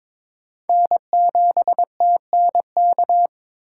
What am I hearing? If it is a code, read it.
Morse code, heard as a keyed tone, about 22 wpm, 700 Hz. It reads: N7TNK
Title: USA Call Signs - 5 Characters - 4x Character Spacing - 22wpm